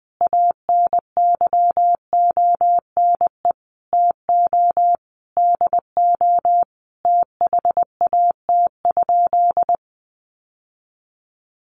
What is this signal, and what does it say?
Morse code, heard as a keyed tone, about 20 wpm, 700 Hz. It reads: ANYONE TO DO THAT?
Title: Sentences from Top 900 Words 20wpm